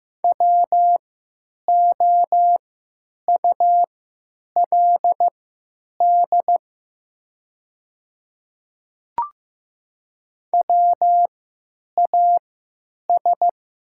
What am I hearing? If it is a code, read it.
Morse code, heard as a keyed tone, about 15 wpm, spaced as for 9 wpm, 700 Hz. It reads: WOULD WAS
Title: 3x Character Spacing - Top 100 Words - 15wpm